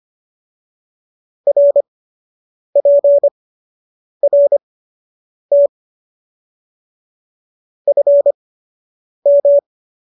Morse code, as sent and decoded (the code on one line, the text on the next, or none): .-. .--. .-. - / ..-. --
RPRT FM